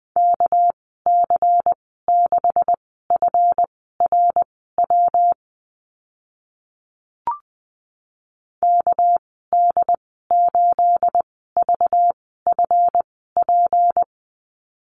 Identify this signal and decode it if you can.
Morse code, heard as a keyed tone, about 20 wpm, spaced as for 14 wpm, 700 Hz. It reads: KC6FRW KD8VFP
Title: 213 Lesson 34g - Number 8 - Call Signs 20wpm